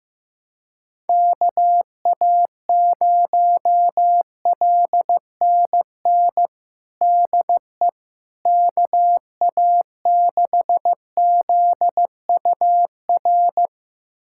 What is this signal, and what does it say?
Morse code, heard as a keyed tone, about 15 wpm, 700 Hz. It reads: KA0LNN DE KA6ZUR
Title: USA Callsign DE Callsign - 6 Characters - 15wpm